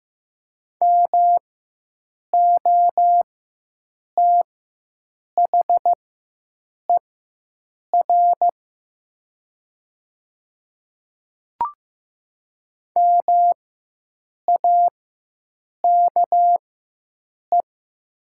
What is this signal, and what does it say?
Morse code, heard as a keyed tone, about 15 wpm, spaced as for 7 wpm, 700 Hz. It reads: MOTHER MAKE
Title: 4x Character Spacing - Top 200 Words - 15wpm